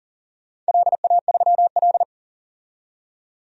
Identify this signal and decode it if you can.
Morse code, heard as a keyed tone, about 41 wpm, 700 Hz. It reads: LA3L
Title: DXCC-100 Call Signs - 4 Characters - 40wpm